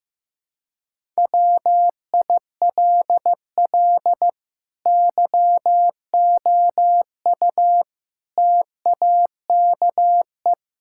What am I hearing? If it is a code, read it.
Morse code, heard as a keyed tone, about 15 wpm, 700 Hz. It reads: WILL YOU TAKE
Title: Sentences from Top 1000 Words - Review Part 6 - 15wpm